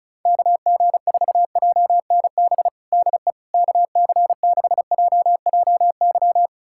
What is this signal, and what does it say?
Morse code, heard as a keyed tone, about 34 wpm, 700 Hz. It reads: KG4JNB DE KC6JJY
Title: USA Callsign DE Callsign 35wpm